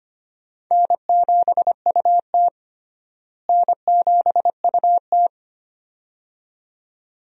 Morse code, heard as a keyed tone, about 25 wpm, 700 Hz. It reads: N7UT N7UT
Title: USA Call Signs - 4 Characters - Repeated 2x - 25wpm